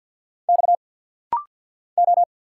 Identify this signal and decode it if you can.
Morse code, heard as a keyed tone, about 50 wpm, 700 Hz. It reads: X O
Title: Single Letter-Number Alt - Rapid-Fire 50wpm